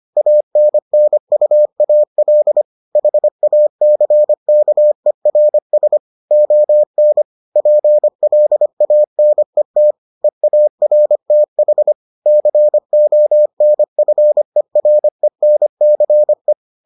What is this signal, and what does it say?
Morse code, heard as a keyed tone, about 25 wpm, 600 Hz. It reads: ANNUAL HACKERS ON PLANET EARTH CONFERENCE